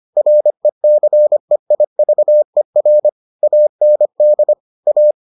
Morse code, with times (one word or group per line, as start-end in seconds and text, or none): 0.2-3.1 RECEIVER
3.4-4.5 AND
4.9-5.1 A